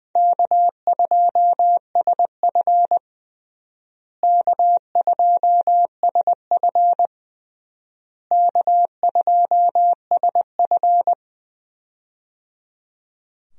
Morse code, heard as a keyed tone, about 20 wpm, 700 Hz. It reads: K2SF K2SF K2SF
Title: USA Call Signs - 4 Characters - Repeated 3x - 20wpm